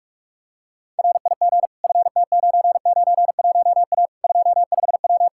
Morse code, heard as a keyed tone, about 46 wpm, 700 Hz. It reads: RIG FT991A 25W
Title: Practice QSO - Exchange 2 - Info - 45wpm